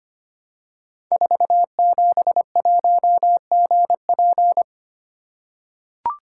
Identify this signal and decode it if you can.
Morse code, heard as a keyed tone, about 25 wpm, 700 Hz. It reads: 471GP
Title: Code Groups - 5 Letter:Numbers - 25wpm